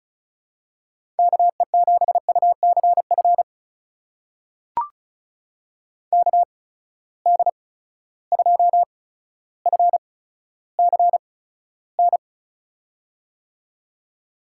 Morse code, heard as a keyed tone, about 36 wpm, spaced as for 15 wpm, 700 Hz. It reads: KE7UCF KD2FCN
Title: USA Call Signs - 6 Characters - 8x Character Spacing - 35wpm